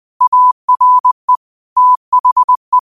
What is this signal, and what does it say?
Morse code, heard as a keyed tone, about 20 wpm, 1000 Hz. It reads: ARE THE